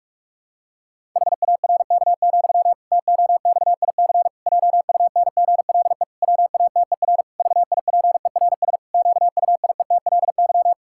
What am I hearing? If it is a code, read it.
Morse code, heard as a keyed tone, about 46 wpm, 700 Hz. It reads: HARK, TOXIC JUNGLE WATER VIPERS QUIETLY
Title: Pangram Challenge v2 - 45wpm